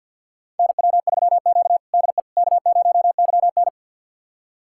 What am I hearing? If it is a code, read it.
Morse code, heard as a keyed tone, about 53 wpm, 700 Hz. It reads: NW3Q DE K0YN
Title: USA Callsign DE Callsign - 4 Characters - 50wpm